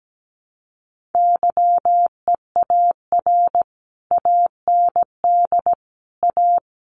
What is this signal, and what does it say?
Morse code, heard as a keyed tone, about 17 wpm, 700 Hz. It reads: YEAR AND A
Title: Sets of 4 Words 17wpm